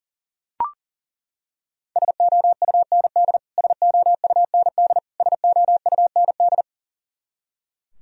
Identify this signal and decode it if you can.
Morse code, heard as a keyed tone, about 40 wpm, 700 Hz. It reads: SOUND SOUND SOUND
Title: Top 100 Words Repeated 40wpm